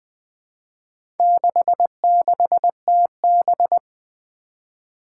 Morse code, heard as a keyed tone, about 21 wpm, 700 Hz. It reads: 66TB
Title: Four Challenging Letter-Number Combinations 20wpm